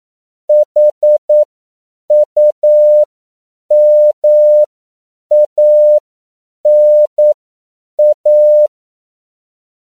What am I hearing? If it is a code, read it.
Morse code, heard as a keyed tone, about 9 wpm, spaced as for 7 wpm, 600 Hz. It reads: HUMANA